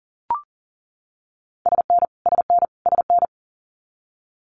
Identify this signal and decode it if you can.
Morse code, heard as a keyed tone, about 40 wpm, 700 Hz. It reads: SN SN SN
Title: Multi-letter Phonemes Repeated 3x 40wpm